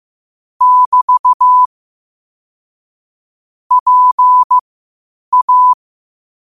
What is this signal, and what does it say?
Morse code, heard as a keyed tone, about 15 wpm, spaced as for 8 wpm, 1000 Hz. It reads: <BT> PA
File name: qotdfiles / 8wpm